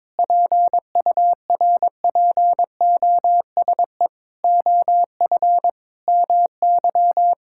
Morse code, heard as a keyed tone, about 22 wpm, 700 Hz. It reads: PURPOSE OF MY
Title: Sentences from Top 1000 Words - Review Part 6 - 22wpm